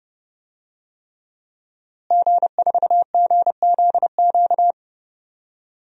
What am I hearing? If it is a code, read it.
Morse code, heard as a keyed tone, about 30 wpm, 700 Hz. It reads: G4GZQ
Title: DXCC-100 Call Signs - 5 Characters - Repeated 3x - 30wpm